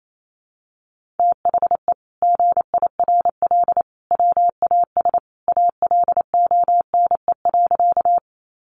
Morse code, heard as a keyed tone, about 28 wpm, 700 Hz. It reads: THE GIRL WAS ALONE.
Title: Sentences from Top 1000 Words 28wpm